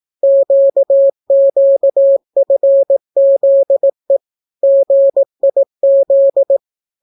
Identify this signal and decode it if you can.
Morse code, heard as a keyed tone, about 18 wpm, 550 Hz. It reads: QQFZE GIZ